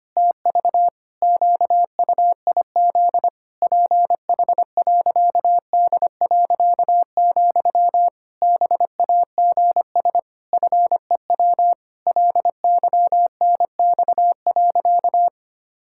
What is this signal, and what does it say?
Morse code, heard as a keyed tone, about 25 wpm, 700 Hz. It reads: TV QUIZ PH.D., BAGS FEW LYNX.
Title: Pangram Challenge v2 - 25wpm